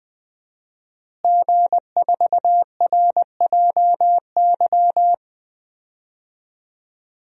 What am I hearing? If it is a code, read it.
Morse code, heard as a keyed tone, about 20 wpm, 700 Hz. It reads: G4RJY
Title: DXCC-100 Call Signs - 5 Characters - Repeated 3x - 20wpm